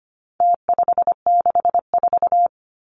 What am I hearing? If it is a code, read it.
Morse code, heard as a keyed tone, about 25 wpm, 700 Hz. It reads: T564
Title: Numbers - Contest Style - 4-Digit 25wpm